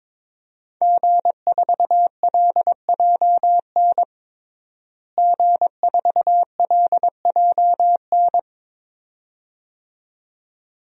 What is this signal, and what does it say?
Morse code, heard as a keyed tone, about 21 wpm, 700 Hz. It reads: G4LJN G4LJN
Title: DXCC-100 Call Signs - 5 Characters - Repeated 2x - 22wpm